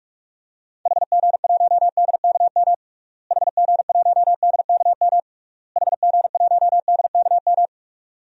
Morse code, heard as a keyed tone, about 46 wpm, 700 Hz. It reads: HG1DKM HG1DKM HG1DKM
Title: DXCC-100 Call Signs - 6 Characters - Repeated 3x - 45wpm